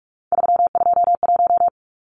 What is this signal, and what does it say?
Morse code, heard as a keyed tone, about 44 wpm, 700 Hz. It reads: 321
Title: Numbers - 3-Digit 45wpm